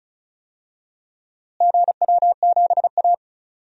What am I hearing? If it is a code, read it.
Morse code, heard as a keyed tone, about 36 wpm, 700 Hz. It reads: GW7A
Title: DXCC-100 Call Signs - 4 Characters - Repeated 2x - 35wpm